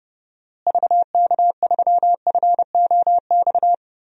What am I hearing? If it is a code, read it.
Morse code, heard as a keyed tone, about 30 wpm, 700 Hz. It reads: VK3FOX
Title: DXCC-100 Call Signs - 6 Characters - Repeated 3x - 30wpm